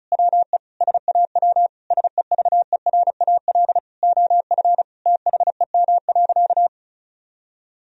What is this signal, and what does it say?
Morse code, heard as a keyed tone, about 36 wpm, 700 Hz. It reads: WE SAW SEVERAL OF THEM.
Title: Sentences from Top 1000 Words - Review Part 2 - 35wpm